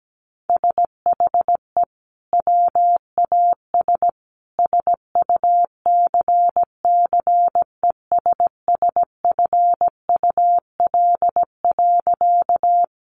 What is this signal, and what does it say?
Morse code, heard as a keyed tone, about 17 wpm, 700 Hz. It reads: SHE WAS SUCCESSFUL.
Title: Sentences from Top 900 Words 17wpm